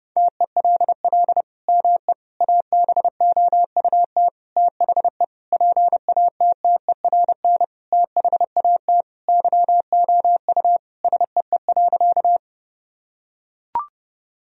Morse code, heard as a keyed tone, about 32 wpm, 700 Hz. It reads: TELL ME ABOUT THE PATTERN THAT YOU SEE.
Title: Sentences from Top 1000 Words - Review Part 2 - 30wpm